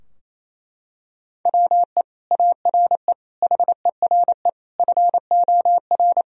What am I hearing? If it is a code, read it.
Morse code, heard as a keyed tone, about 28 wpm, 700 Hz. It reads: WE ARE HERE FOR
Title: Sentences from Top 1000 Words - Review Part 4 - 28wpm